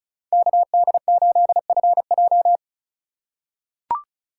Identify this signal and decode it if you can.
Morse code, heard as a keyed tone, about 34 wpm, 700 Hz. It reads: KD8FJ
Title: USA Call Signs - 5 Characters - 4x Character Spacing - 35wpm